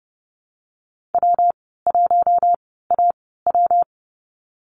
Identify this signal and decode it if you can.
Morse code, heard as a keyed tone, about 30 wpm, spaced as for 17 wpm, 700 Hz. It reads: W1AW
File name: course / 30:10wpm / 116 Lesson 19g - Letter w - Call Signs 30wpm